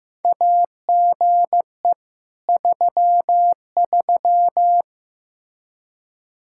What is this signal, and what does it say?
Morse code, heard as a keyed tone, about 15 wpm, 700 Hz. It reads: AGE 33
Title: K1USN SST QSO Practice 15wpm